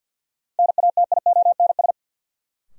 Morse code, heard as a keyed tone, about 46 wpm, 700 Hz. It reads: NATIONS
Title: Sets of 3 Words - New Word Drill - 50wpm